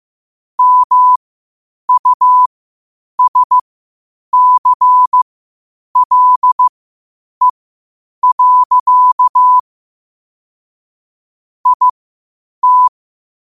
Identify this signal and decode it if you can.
Morse code, heard as a keyed tone, about 15 wpm, spaced as for 8 wpm, 1000 Hz. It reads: MUSCLE. IT
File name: qotdfiles / 8wpm